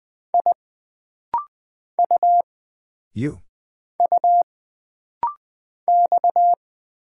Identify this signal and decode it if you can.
Morse code, heard as a keyed tone, about 20 wpm, 700 Hz. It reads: I U U X
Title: Single Character (Letter:Number:Punctuation:Prosign) - Rapid-Fire - 20wpm